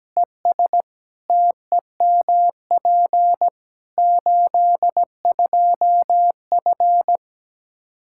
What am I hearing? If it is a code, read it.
Morse code, heard as a keyed tone, about 17 wpm, 700 Hz. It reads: ES TEMP 82F
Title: K1USN SST QSO Practice 17wpm